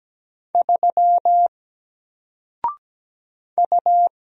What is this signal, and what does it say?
Morse code, heard as a keyed tone, about 17 wpm, 700 Hz. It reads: 3 U
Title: Single Character (Letter:Number:Punctuation:Prosign) Alt 17wpm